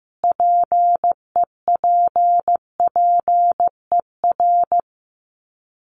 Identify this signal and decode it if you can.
Morse code, heard as a keyed tone, about 15 wpm, 700 Hz. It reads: PEPPER
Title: Sets of 5 Words 15wpm